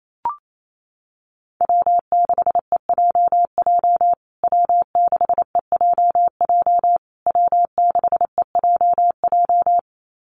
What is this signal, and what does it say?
Morse code, heard as a keyed tone, about 28 wpm, 700 Hz. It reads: W6EJJ W6EJJ W6EJJ
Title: USA Call Signs Repeated 3x 28wpm